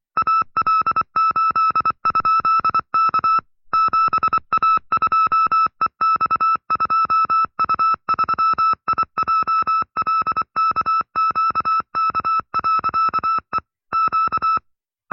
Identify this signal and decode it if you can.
Morse code, heard as a keyed tone, about 24 wpm, 1310 Hz. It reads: AL8?K 7A2EX2U3IJLKQK.E Q
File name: FreeFiles / CW-practice / 20WPM-side2